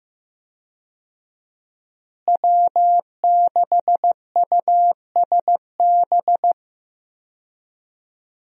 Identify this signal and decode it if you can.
Morse code, heard as a keyed tone, about 15 wpm, 700 Hz. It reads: W6USB
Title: USA Call Signs - 5 Characters - Repeated 2x - 15wpm